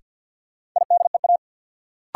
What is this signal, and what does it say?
Morse code, heard as a keyed tone, about 53 wpm, 700 Hz. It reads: IDEA
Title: Sets of 3 Words - New Word Drill - 50wpm